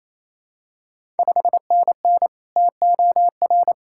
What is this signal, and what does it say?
Morse code, heard as a keyed tone, about 28 wpm, 700 Hz. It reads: 5NN TOR
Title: Ontario QSO Party Exchange 28wpm